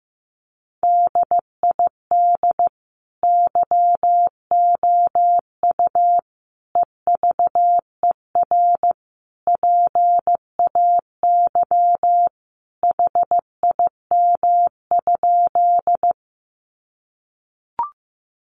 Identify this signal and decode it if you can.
Morse code, heard as a keyed tone, about 15 wpm, 700 Hz. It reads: DID YOU EVER PAY HIM?
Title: Sentences from Top 700 Words 15wpm